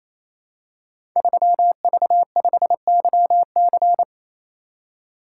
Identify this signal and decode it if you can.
Morse code, heard as a keyed tone, about 27 wpm, 700 Hz. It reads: 3V5YC
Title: Code Groups - 5 Letter:Numbers - 28wpm